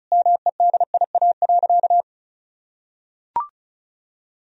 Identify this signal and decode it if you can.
Morse code, heard as a keyed tone, about 34 wpm, 700 Hz. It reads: MEDIA.
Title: Sentences from Top 1000 Words - Review Part 4 - 35wpm